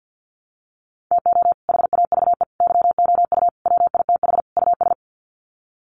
Morse code, heard as a keyed tone, about 50 wpm, 700 Hz. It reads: TO HAVE YOU WITH US
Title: Sets of 5 Words 50wpm